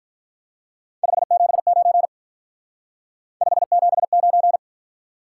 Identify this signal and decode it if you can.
Morse code, heard as a keyed tone, about 43 wpm, 700 Hz. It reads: 579 579
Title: Speed-Racing Top 180 QSO Elements 35wpm